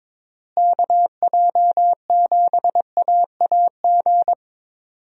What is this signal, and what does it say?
Morse code, heard as a keyed tone, about 22 wpm, 700 Hz. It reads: KJ7AAG
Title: USA Call Signs - 6 Characters - Repeated 2x - 22wpm